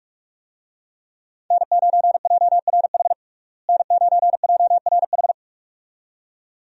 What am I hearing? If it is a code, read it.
Morse code, heard as a keyed tone, about 42 wpm, 700 Hz. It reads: N9JRH N9JRH
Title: USA Call Signs - 5 Characters - Repeated 2x - 45wpm